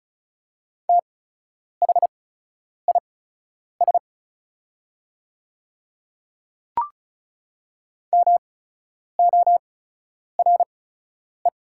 Morse code, heard as a keyed tone, about 34 wpm, spaced as for 10 wpm, 690 Hz. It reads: THIS MORE
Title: 8x Character Spacing - Top 100 Words - 35wpm